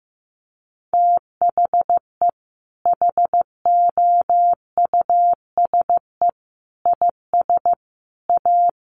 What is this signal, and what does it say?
Morse code, heard as a keyed tone, about 15 wpm, 700 Hz. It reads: THE HOUSE IS A
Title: Sentences from Top 1500 Words 15wpm